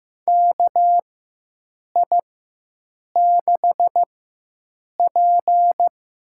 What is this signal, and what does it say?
Morse code, heard as a keyed tone, about 15 wpm, spaced as for 7 wpm, 700 Hz. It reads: KI6P
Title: USA Call Signs - 6 Characters - 4x Character Spacing - 15wpm